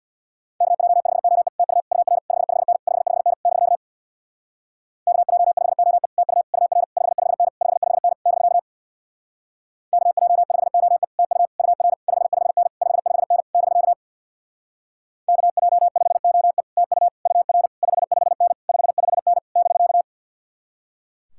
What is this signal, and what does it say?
Morse code, heard as a keyed tone, about 64 wpm, 700 Hz. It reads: KJ5OE TU UR 55N 55N <BK> KJ5OE TU UR 55N 55N <BK> KJ5OE TU UR 55N 55N <BK> KJ5OE TU UR 55N 55N <BK>